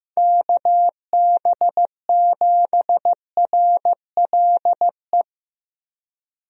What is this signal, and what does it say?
Morse code, heard as a keyed tone, about 15 wpm, 700 Hz. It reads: KB7RLE
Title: USA Call Signs - 6 Characters - 15wpm